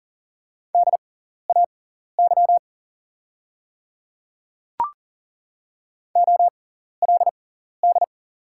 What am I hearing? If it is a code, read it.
Morse code, heard as a keyed tone, about 41 wpm, spaced as for 14 wpm, 690 Hz. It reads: DAY OLD
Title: 6x Character Spacing - Top 200 Words - 40wpm